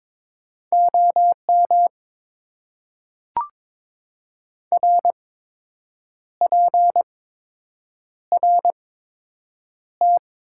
Morse code, heard as a keyed tone, about 22 wpm, spaced as for 7 wpm, 700 Hz. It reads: OM RPRT
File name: Top 90 QSO Elements - 8x Character Spacing - 22wpm